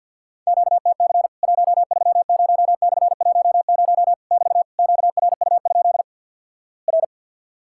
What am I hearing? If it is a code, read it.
Morse code, heard as a keyed tone, about 50 wpm, 700 Hz. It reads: QTC 130/10 <BT> QRU? R
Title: Worked All Europe DX Contest - QTC Exchange with Repeats - 50wpm